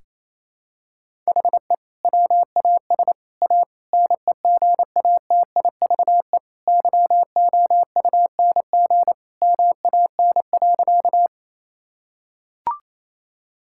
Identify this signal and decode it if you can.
Morse code, heard as a keyed tone, about 28 wpm, 700 Hz. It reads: HE WAS A NEGATIVE YOUNG MAN.
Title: Sentences from Top 1000 Words - Review Part 6 - 28wpm